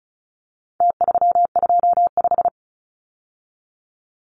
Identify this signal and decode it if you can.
Morse code, heard as a keyed tone, about 34 wpm, 700 Hz. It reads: T325
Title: Numbers - Contest Style - 4-Digit 35wpm